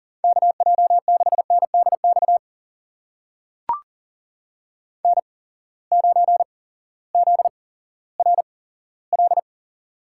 Morse code, heard as a keyed tone, about 40 wpm, spaced as for 18 wpm, 700 Hz. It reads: KJ6NDX N9ZRL
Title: USA Call Signs - 8x Character Spacing - 40wpm